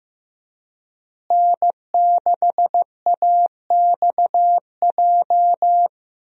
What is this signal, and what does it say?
Morse code, heard as a keyed tone, about 15 wpm, 700 Hz. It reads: N6AXJ
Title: USA Call Signs - 5 Characters - Repeated 2x - 15wpm